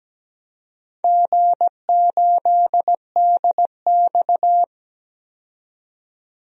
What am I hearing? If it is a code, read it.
Morse code, heard as a keyed tone, about 17 wpm, 700 Hz. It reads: G8DX
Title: DXCC-100 Call Signs - 4 Characters - 17wpm